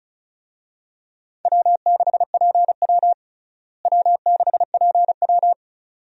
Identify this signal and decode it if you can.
Morse code, heard as a keyed tone, about 36 wpm, 700 Hz. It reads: W6PW W6PW
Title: USA Call Signs - 4 Characters - Repeated 2x - 35wpm